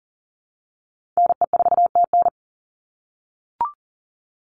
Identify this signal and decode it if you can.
Morse code, heard as a keyed tone, about 40 wpm, 700 Hz. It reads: NE4TN
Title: NA SOTA Chaser Callsigns 40wpm